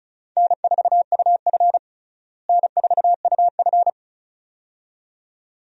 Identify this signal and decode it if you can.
Morse code, heard as a keyed tone, about 35 wpm, 700 Hz. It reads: N4UF N4UF
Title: USA Call Signs - 4 Characters - Repeated 2x - 35wpm